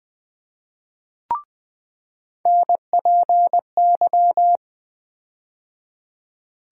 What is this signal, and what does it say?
Morse code, heard as a keyed tone, about 20 wpm, 700 Hz. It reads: NPY
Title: USA Call Sign Postfixes - 3 Letters 20wpm